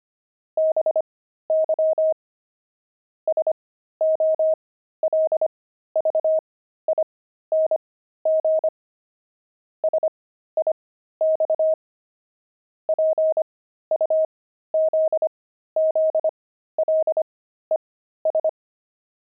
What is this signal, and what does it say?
Morse code, heard as a keyed tone, about 25 wpm, spaced as for 13 wpm, 630 Hz. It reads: BY SOLVING SIX PUZZLES
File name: pangram-fs-13-cs-25-puzzles